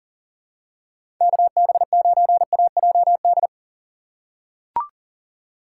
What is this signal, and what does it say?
Morse code, heard as a keyed tone, about 38 wpm, 700 Hz. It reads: KB9AJD
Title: USA Call Signs - 4x Character Spacing - 40wpm